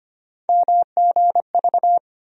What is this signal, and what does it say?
Morse code, heard as a keyed tone, about 25 wpm, 700 Hz. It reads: MGV
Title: USA Call Sign Postfixes - 3 Letters 25wpm